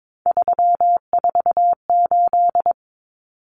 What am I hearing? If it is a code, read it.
Morse code, heard as a keyed tone, about 22 wpm, 700 Hz. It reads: 348
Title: Common Cut Numbers - 3-Digit 22wpm